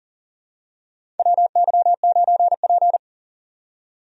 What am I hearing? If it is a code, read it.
Morse code, heard as a keyed tone, about 40 wpm, 700 Hz. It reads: WY9P
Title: USA Call Signs - 4 Characters - 40wpm